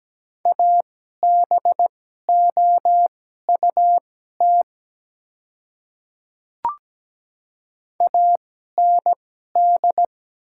Morse code, heard as a keyed tone, about 17 wpm, spaced as for 12 wpm, 700 Hz. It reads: ABOUT AND
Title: 2x Character Spacing - Top 100 Words - 17wpm